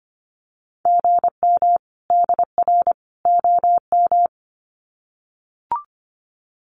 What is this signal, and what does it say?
Morse code, heard as a keyed tone, about 25 wpm, 700 Hz. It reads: GM DR OM
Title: Common QSO Phrases 25wpm